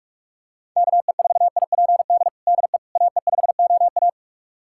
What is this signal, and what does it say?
Morse code, heard as a keyed tone, about 46 wpm, 690 Hz. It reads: KE4IPD DE AE5OA
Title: USA Callsign DE Callsign 45wpm